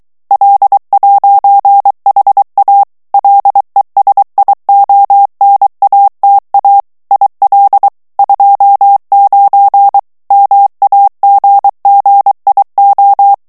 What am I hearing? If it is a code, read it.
Morse code, heard as a keyed tone, about 23 wpm, 800 Hz. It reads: L'HA LESIONATA IL 29 MAGGIO
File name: ITALIEN texte clair 1 vitesse 22 mots minute